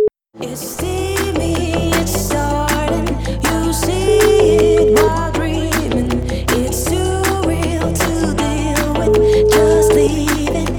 Ringback tone (Europe, 425 Hz), with noise.